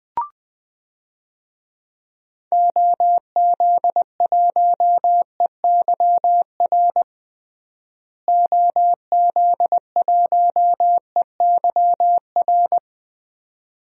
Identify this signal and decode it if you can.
Morse code, heard as a keyed tone, about 20 wpm, 700 Hz. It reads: OZ1EYR OZ1EYR
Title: DXCC-100 Call Signs - 6 Characters - Repeated 2x - 20wpm